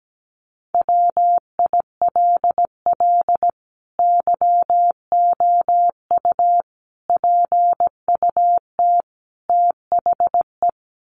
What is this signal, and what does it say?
Morse code, heard as a keyed tone, about 17 wpm, 700 Hz. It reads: WILL YOU PUT THE
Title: Sentences from Top 1400 Words 17wpm